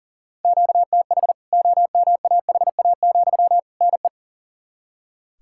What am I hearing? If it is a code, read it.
Morse code, heard as a keyed tone, about 40 wpm, 700 Hz. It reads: QTH OMAHA, NE
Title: K1USN SST QSO Practice 40wpm